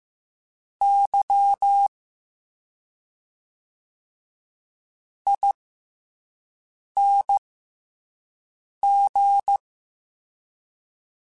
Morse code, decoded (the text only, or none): YING